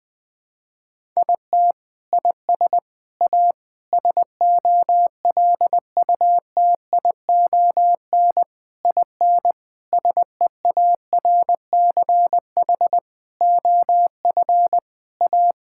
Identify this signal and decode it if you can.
Morse code, heard as a keyed tone, about 20 wpm, 700 Hz. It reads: IT IS A SOLUTION IN SEARCH OF A
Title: Sentences with 7-Word Phrases - Spoken Beginning - 20wpm